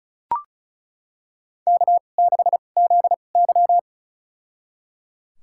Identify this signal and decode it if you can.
Morse code, heard as a keyed tone, about 36 wpm, spaced as for 25 wpm, 700 Hz. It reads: K6ZY
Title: USA Call Signs - 4 Characters - 2x Character Spacing - 35wpm